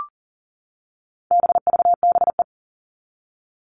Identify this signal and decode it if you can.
Morse code, heard as a keyed tone, about 40 wpm, 700 Hz. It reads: BVBE
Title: All Cut Numbers - 4-Digit 40wpm